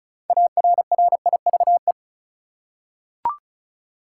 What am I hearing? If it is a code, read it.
Morse code, heard as a keyed tone, about 35 wpm, 700 Hz. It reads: ARRIVE
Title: Six Letter Words - 35wpm